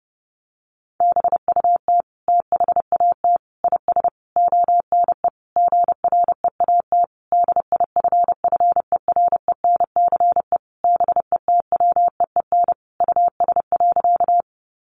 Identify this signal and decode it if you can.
Morse code, heard as a keyed tone, about 30 wpm, 700 Hz. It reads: BUT THAT IS ONE GREAT DIFFERENCE BETWEEN US.
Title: Sentences from Top 800 Words 30wpm